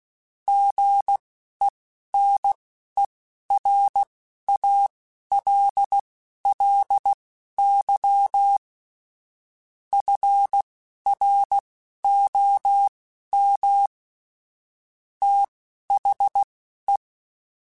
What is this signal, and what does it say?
Morse code, heard as a keyed tone, about 16 wpm, spaced as for 11 wpm, 800 Hz. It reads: GENERALLY FROM THE
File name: Weather Sets / Weather Sets-mp3s / Weather 5 5 @10